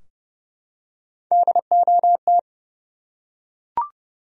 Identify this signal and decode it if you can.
Morse code, heard as a keyed tone, about 30 wpm, 700 Hz. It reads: DOT
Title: USA Common Names - Max 5 Letters - 30wpm